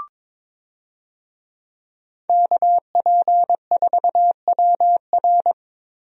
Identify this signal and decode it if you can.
Morse code, heard as a keyed tone, about 22 wpm, 700 Hz. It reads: KP4WR
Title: USA Call Signs - 5 Characters - Repeated 2x - 22wpm